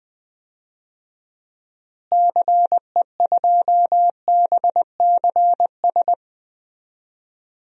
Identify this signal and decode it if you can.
Morse code, heard as a keyed tone, about 20 wpm, 700 Hz. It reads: CE2BCS
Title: DXCC-100 Call Signs - 6 Characters - Repeated 2x - 20wpm